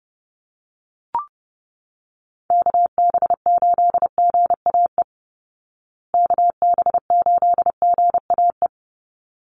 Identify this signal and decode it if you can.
Morse code, heard as a keyed tone, about 28 wpm, 700 Hz. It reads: KB8GAE KB8GAE
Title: Call Signs Repeated 3x - 30wpm